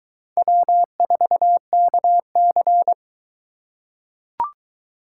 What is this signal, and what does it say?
Morse code, heard as a keyed tone, about 20 wpm, 700 Hz. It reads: W4KC